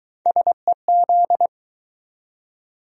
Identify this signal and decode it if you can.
Morse code, heard as a keyed tone, about 20 wpm, 700 Hz. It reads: SEZ